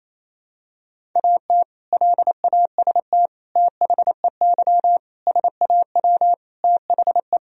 Morse code, heard as a keyed tone, about 28 wpm, 700 Hz. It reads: AT LAST THEY SAW THE
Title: Sentences from Top 1000 Words - Review Part 1 - 28wpm